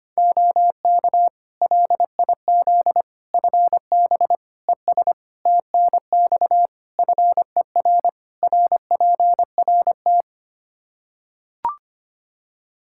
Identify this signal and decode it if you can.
Morse code, heard as a keyed tone, about 25 wpm, 700 Hz. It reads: OK LIZ FB ES TNX FER RPRT
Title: K1USN SST QSO Practice 25wpm